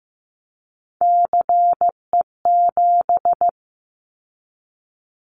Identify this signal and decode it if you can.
Morse code, heard as a keyed tone, about 15 wpm, 700 Hz. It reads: CE7
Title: Callsign Prefixes for Top 100 DXCC - 15wpm